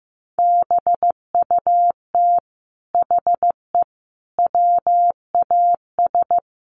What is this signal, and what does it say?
Morse code, heard as a keyed tone, about 15 wpm, 700 Hz. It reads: BUT HE WAS
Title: Sentences from Top 1500 Words 15wpm